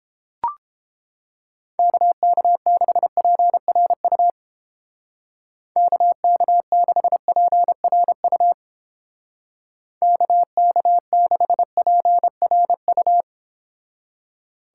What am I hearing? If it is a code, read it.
Morse code, heard as a keyed tone, about 29 wpm, 700 Hz. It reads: KK6PRU KK6PRU KK6PRU